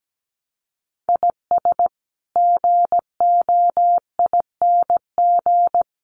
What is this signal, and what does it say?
Morse code, heard as a keyed tone, about 17 wpm, 700 Hz. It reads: IS GOING
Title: Sets of 2 Words 17wpm